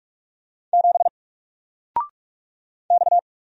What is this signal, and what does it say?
Morse code, heard as a keyed tone, about 46 wpm, 700 Hz. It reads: 7 X
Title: Single Letter-Number Alt 45wpm